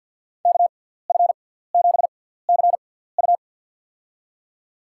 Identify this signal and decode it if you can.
Morse code, heard as a keyed tone, about 52 wpm, spaced as for 17 wpm, 690 Hz. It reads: KF7CU
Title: USA Call Signs - 5 Characters - 6x Character Spacing - 50wpm